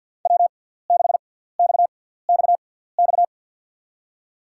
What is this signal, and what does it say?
Morse code, heard as a keyed tone, about 53 wpm, spaced as for 17 wpm, 690 Hz. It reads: W6XXX
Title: USA Call Signs - 6x Character Spacing - 50wpm